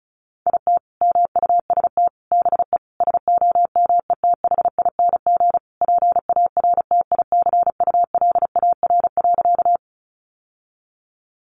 Morse code, heard as a keyed tone, about 34 wpm, 700 Hz. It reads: IT MUST BE SOMETHING PARTICULAR.
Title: Sentences from Top 600 Words 35wpm